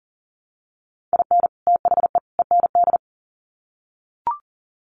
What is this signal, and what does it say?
Morse code, heard as a keyed tone, about 40 wpm, 700 Hz. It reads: IN THE END
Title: Sets of 3 Words 40wpm